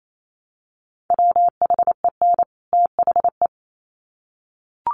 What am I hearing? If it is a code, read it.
Morse code, heard as a keyed tone, about 28 wpm, 700 Hz. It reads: WHEN THE
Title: Sets of 2 Words 28wpm